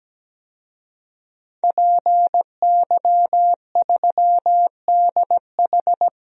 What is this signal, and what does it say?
Morse code, heard as a keyed tone, about 17 wpm, 700 Hz. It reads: PY3DH
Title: DXCC-100 Call Signs - 5 Characters - 17wpm